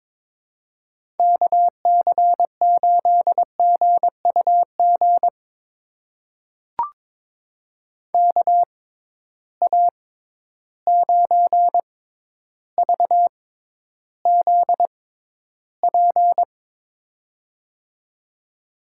Morse code, heard as a keyed tone, about 22 wpm, spaced as for 11 wpm, 700 Hz. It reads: KC8GUG KA9VZP
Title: USA Call Signs - 6 Characters - 6x Character Spacing - 22wpm